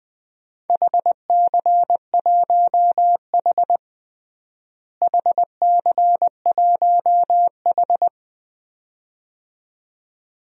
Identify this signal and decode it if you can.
Morse code, heard as a keyed tone, about 20 wpm, 700 Hz. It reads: HC1H HC1H
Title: DXCC-100 Call Signs - 4 Characters - Repeated 2x - 20wpm